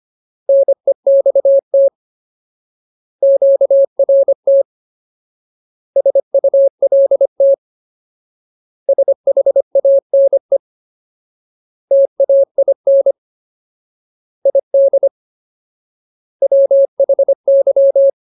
Morse code, heard as a keyed tone, about 25 wpm, 550 Hz. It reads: NEXT QRT SULT SHANE TAIN ID WHY
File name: Ultimate Word List / 25WPM Ultimate